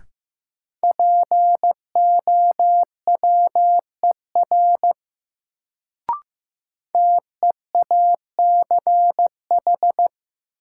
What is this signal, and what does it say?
Morse code, heard as a keyed tone, about 16 wpm, 700 Hz. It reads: POWER TEACH
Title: Sentences from Top 500 Words - New Word Drill - 15wpm